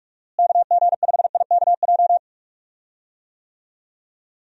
Morse code, heard as a keyed tone, about 43 wpm, 700 Hz. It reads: KG5IKJ
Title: USA Call Signs - 6 Characters - Repeated 2x - 45wpm